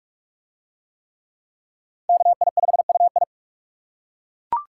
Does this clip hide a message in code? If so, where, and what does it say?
Yes: Morse code from 2.1 to 3.2 s, heard as a keyed tone, about 40 wpm, 700 Hz. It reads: KI5UI